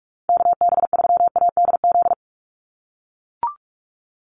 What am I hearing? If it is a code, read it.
Morse code, heard as a keyed tone, about 44 wpm, 700 Hz. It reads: KB3ADZ
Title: USA Call Signs 45wpm